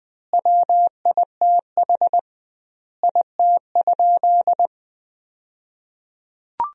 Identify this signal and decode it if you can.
Morse code, heard as a keyed tone, about 20 wpm, 700 Hz. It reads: WITH IT?
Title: 2x Word-Spacing - Sentences from Top 100 Words - 20wpm